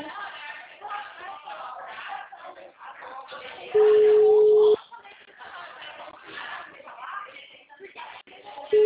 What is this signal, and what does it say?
Ringback tone (Europe, 425 Hz), with speech.
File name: out- -2003- - - .35806